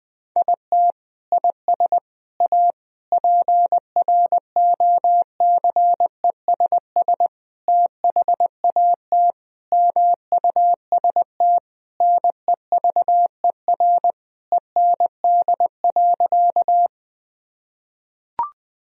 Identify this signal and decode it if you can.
Morse code, heard as a keyed tone, about 20 wpm, 700 Hz. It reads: IT IS A PROCESS THAT MUST NEVER END.
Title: Sentences from Top 1000 Words - Review Part 3 - 20wpm